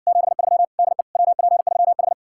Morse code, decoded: 73 DE WP3H

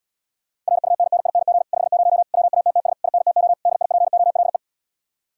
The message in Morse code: RUNNING 50 WATTS INTO DIPOLE

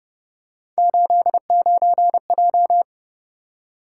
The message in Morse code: 89J

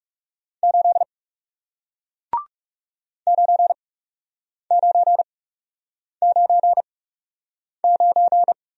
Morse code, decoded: 8 9999

